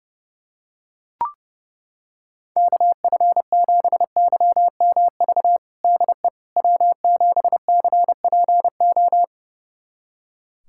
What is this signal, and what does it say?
Morse code, heard as a keyed tone, about 32 wpm, 690 Hz. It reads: KF7YMV DE W7CPO
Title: USA Callsign DE Callsign 30wpm